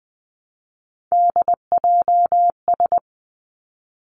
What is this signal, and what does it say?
Morse code, heard as a keyed tone, about 20 wpm, 700 Hz. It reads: DJS
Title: Three Challenging Letter-Number Combinations 20wpm